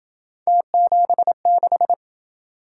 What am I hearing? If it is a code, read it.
Morse code, heard as a keyed tone, about 27 wpm, 700 Hz. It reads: T76